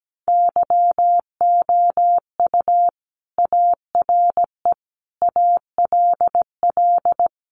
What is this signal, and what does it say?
Morse code, heard as a keyed tone, about 17 wpm, 700 Hz. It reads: YOU ARE ALL
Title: Sentences from Top 900 Words 17wpm